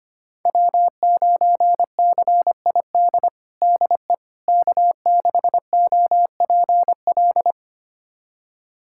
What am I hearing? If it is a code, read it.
Morse code, heard as a keyed tone, about 25 wpm, 700 Hz. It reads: W9CID DE K6OPL
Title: USA Callsign DE Callsign - 5 Characters - 25wpm